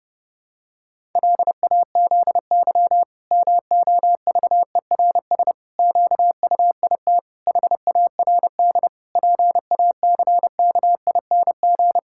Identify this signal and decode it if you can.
Morse code, heard as a keyed tone, about 30 wpm, 700 Hz. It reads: LAZY MOVERS QUIT HARD PACKING
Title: Pangram Challenge v2 - 30wpm